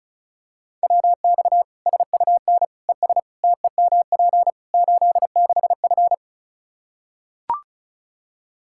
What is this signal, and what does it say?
Morse code, heard as a keyed tone, about 34 wpm, 700 Hz. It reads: WX SUN ES TEMP 86F
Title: K1USN SST QSO Practice 35wpm